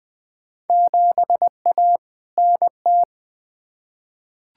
Morse code, heard as a keyed tone, about 20 wpm, 700 Hz. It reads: 7A NT
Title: Field Day Exchanges Encore - 20wpm